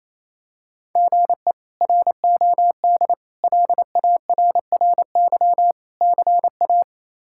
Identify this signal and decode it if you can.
Morse code, heard as a keyed tone, about 27 wpm, 700 Hz. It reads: GE ROD LARRY CA
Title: K1USN SST QSO Practice - 28wpm